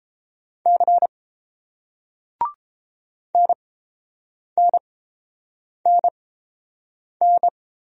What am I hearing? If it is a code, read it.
Morse code, heard as a keyed tone, about 30 wpm, spaced as for 7 wpm, 700 Hz. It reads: C NNNN